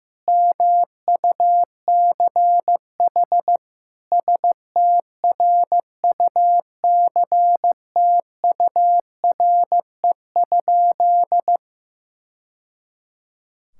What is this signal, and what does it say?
Morse code, heard as a keyed tone, about 15 wpm, 700 Hz. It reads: MUCH STRUCTURE?
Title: Sentences from Top 1000 Words - Review Part 6 - 15wpm